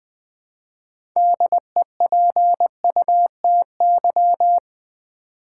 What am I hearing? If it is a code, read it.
Morse code, heard as a keyed tone, about 20 wpm, 700 Hz. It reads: DEPUTY
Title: Build A Word - 2 Letters At A Time - 6-Letter Words - 20wpm